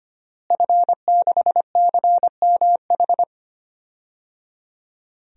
Morse code, heard as a keyed tone, about 25 wpm, 700 Hz. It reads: F6CMH
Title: DXCC-100 Call Signs - 5 Characters - Repeated 3x - 25wpm